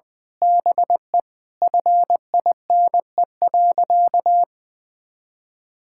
Morse code, heard as a keyed tone, about 19 wpm, 700 Hz. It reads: BE FINE.
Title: Sentences with 7-Word Phrases - Spoken Beginning - 20wpm